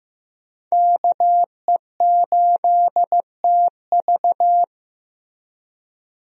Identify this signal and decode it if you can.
Morse code, heard as a keyed tone, about 16 wpm, 700 Hz. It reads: KE8TV
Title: USA Call Signs - 5 Characters - Repeated 2x - 15wpm